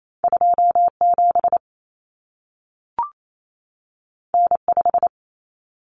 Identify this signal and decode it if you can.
Morse code, heard as a keyed tone, about 28 wpm, 700 Hz. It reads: 27 N5
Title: Common Cut Numbers - 2-Digit 28wpm